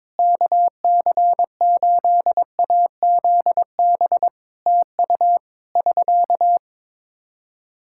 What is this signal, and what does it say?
Morse code, heard as a keyed tone, about 22 wpm, 700 Hz. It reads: KC8AZB TU <SK>
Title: Practice QSO - Exchange 4 - Ending - 22wpm